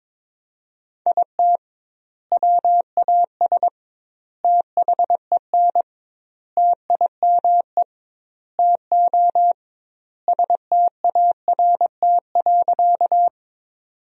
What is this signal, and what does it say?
Morse code, heard as a keyed tone, about 22 wpm, 700 Hz. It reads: IT WAS THEN TIME TO START.
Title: 2x Word-Spacing - Sentences from Top 300 Words - 22wpm